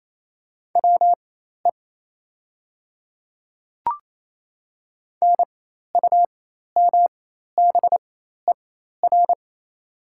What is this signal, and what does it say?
Morse code, heard as a keyed tone, about 28 wpm, spaced as for 13 wpm, 700 Hz. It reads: WE NUMBER
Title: 4x Character Spacing - Top 200 Words - 28wpm